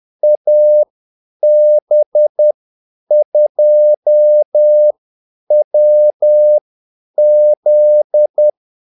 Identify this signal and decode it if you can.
Morse code, heard as a keyed tone, about 10 wpm, spaced as for 8 wpm, 600 Hz. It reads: AB2WZ
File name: Callsigns-10-8wpm